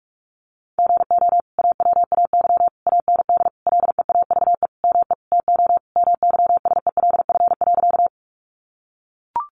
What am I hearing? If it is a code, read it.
Morse code, heard as a keyed tone, about 44 wpm, 700 Hz. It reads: GO AWAY AND LEAVE ME TO MYSELF.
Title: Sentences from Top 1300 Words 45wpm